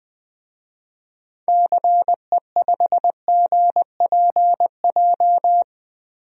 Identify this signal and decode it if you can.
Morse code, heard as a keyed tone, about 20 wpm, 700 Hz. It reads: CE5GPJ
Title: DXCC-100 Call Signs - 6 Characters - 20wpm